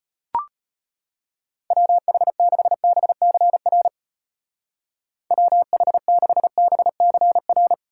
Morse code, heard as a keyed tone, about 34 wpm, 700 Hz. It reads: WH6BCR WH6BCR